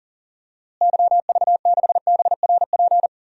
Morse code, heard as a keyed tone, about 40 wpm, 700 Hz. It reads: YV6BRP
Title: DXCC-100 Call Signs - 6 Characters - Repeated 2x - 40wpm